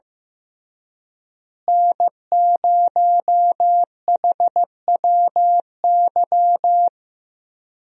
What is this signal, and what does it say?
Morse code, heard as a keyed tone, about 15 wpm, 700 Hz. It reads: N0HWY
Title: USA Call Signs - 5 Characters - Repeated 3x - 15wpm